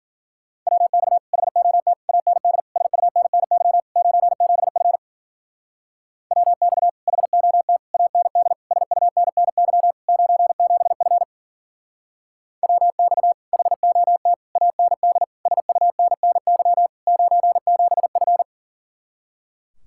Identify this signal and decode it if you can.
Morse code, heard as a keyed tone, about 50 wpm, 700 Hz. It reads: WX HOT AND SUNNY 97F WX HOT AND SUNNY 97F WX HOT AND SUNNY 97F